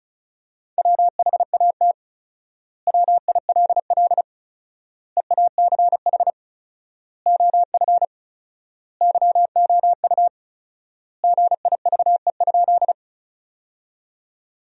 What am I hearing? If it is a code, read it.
Morse code, heard as a keyed tone, about 36 wpm, 700 Hz. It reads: WHAT WILL EACH OF YOU GIVE?
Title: 4x Word-Spacing - Sentences from Top 200 Words - 35wpm